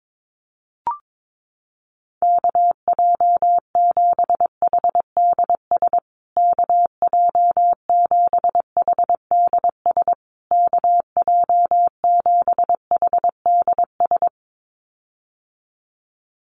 Morse code, heard as a keyed tone, about 22 wpm, 700 Hz. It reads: KJ7HDS KJ7HDS KJ7HDS
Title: USA Call Signs Repeated 3x 22wpm